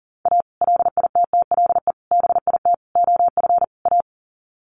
Morse code, heard as a keyed tone, about 40 wpm, 700 Hz. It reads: A LITTLE BIT OF A
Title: Sets of 5 Words 40wpm